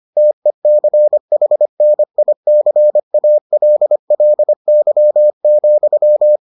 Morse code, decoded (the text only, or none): TECHNICALLY,